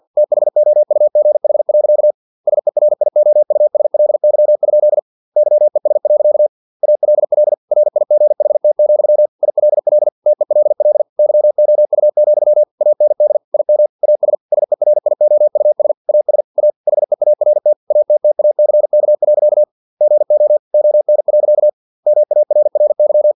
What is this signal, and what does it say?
Morse code, heard as a keyed tone, about 50 wpm, 600 Hz. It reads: THOUGH. SERIOUSLY? YES. ALL RIGHT, ILL TELL YOU, AND IM AS SERIOUS AS A HEART ATTACK. GO ON. MARRY
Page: Never 49wpm